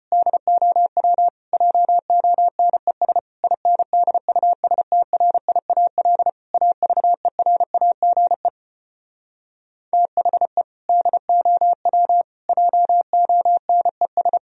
Morse code, heard as a keyed tone, about 33 wpm, 700 Hz. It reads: DOW JONES INDUSTRIAL AVERAGE THE DOW JONES